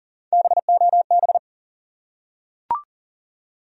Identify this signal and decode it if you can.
Morse code, heard as a keyed tone, about 40 wpm, 700 Hz. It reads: BOB
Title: USA Common Names - Max 3 Letters - 40wpm